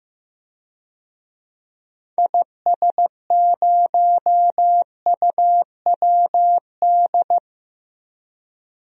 Morse code, heard as a keyed tone, about 15 wpm, 700 Hz. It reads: IS0UWD
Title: DXCC-100 Call Signs - 6 Characters - Repeated 2x - 15wpm